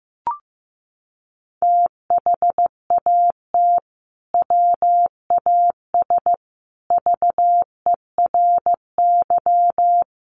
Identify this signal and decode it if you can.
Morse code, heard as a keyed tone, about 15 wpm, 700 Hz. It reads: THAT WAS VERY
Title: Sentences from Top 1200 Words 15wpm